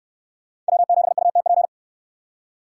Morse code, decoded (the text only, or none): W7UTP